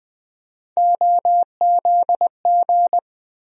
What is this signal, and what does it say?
Morse code, heard as a keyed tone, about 20 wpm, 700 Hz. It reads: OZG
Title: USA Call Sign Postfixes - 3 Letters 20wpm